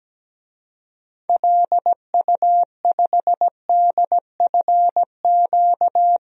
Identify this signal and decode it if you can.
Morse code, heard as a keyed tone, about 17 wpm, 700 Hz. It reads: LU5DFQ
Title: DXCC-100 Call Signs - 6 Characters - 17wpm